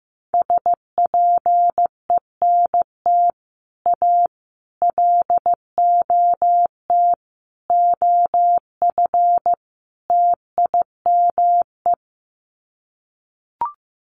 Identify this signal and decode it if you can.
Morse code, heard as a keyed tone, about 15 wpm, 700 Hz. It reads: SPENT A LOT OF TIME
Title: Sets of 5 Words 15wpm